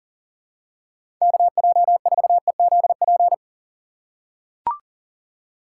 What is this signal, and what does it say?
Morse code, heard as a keyed tone, about 40 wpm, 700 Hz. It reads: KJ4EZP
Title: USA Call Signs - 6 Characters - 4x Character Spacing - 40wpm